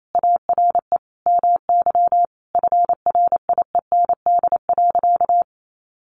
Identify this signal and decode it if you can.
Morse code, heard as a keyed tone, about 28 wpm, 700 Hz. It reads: ARE MY FRIEND.
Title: Sentences from Top 1000 Words 28wpm